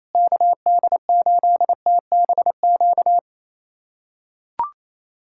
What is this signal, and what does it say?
Morse code, heard as a keyed tone, about 28 wpm, 700 Hz. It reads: KD8TBQ
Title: USA Callsign DE Callsign 28wpm